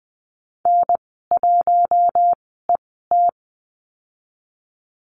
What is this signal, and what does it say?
Morse code, heard as a keyed tone, about 20 wpm, spaced as for 14 wpm, 700 Hz. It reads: N1ET
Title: 069 Lesson 12g - Letter d - Call Signs 20wpm